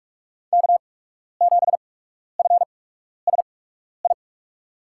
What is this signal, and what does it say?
Morse code, heard as a keyed tone, about 44 wpm, spaced as for 12 wpm, 700 Hz. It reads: K7FSI
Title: USA Call Signs - 5 Characters - 8x Character Spacing - 45wpm